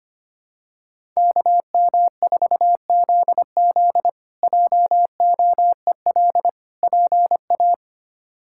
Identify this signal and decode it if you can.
Morse code, heard as a keyed tone, about 25 wpm, 700 Hz. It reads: KM4ZZ JOEL PA
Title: K1USN SST QSO Practice - 25wpm